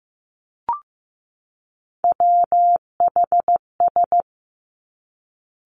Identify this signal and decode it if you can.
Morse code, heard as a keyed tone, about 15 wpm, 700 Hz. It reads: WHS
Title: Three Challenging Letter-Number Combinations 15wpm